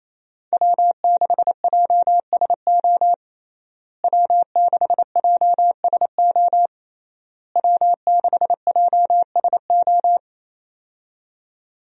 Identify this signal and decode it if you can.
Morse code, heard as a keyed tone, about 28 wpm, 700 Hz. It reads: W6JSO W6JSO W6JSO
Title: USA Call Signs - 5 Characters - Repeated 3x - 28wpm